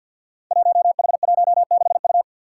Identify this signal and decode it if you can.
Morse code, heard as a keyed tone, about 53 wpm, 700 Hz. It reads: 1H16U
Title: Five Challenging Letter-Number Combinations 50wpm